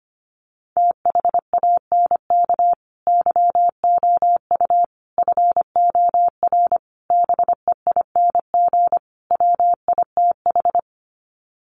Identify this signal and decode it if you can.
Morse code, heard as a keyed tone, about 25 wpm, 700 Hz. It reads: THANK YOU FOR BEING WITH
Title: Sets of 5 Words 25wpm